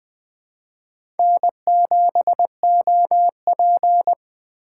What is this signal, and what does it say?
Morse code, heard as a keyed tone, about 20 wpm, 700 Hz. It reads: N7OP
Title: USA Call Signs - 4 Characters - 2x Character Spacing - 20wpm